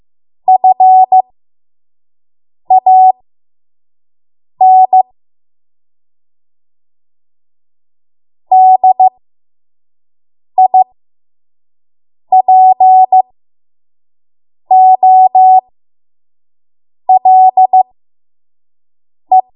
Morse code, heard as a keyed tone, about 15 wpm, spaced as for 5 wpm, 750 Hz. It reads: FAN DIPOLE
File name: Morse / 05WPM